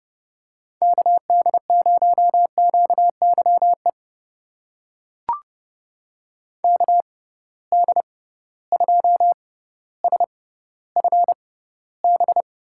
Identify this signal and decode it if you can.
Morse code, heard as a keyed tone, about 30 wpm, spaced as for 15 wpm, 700 Hz. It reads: KD0QYE KD2SFB
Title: USA Call Signs - 6 Characters - 6x Character Spacing - 30wpm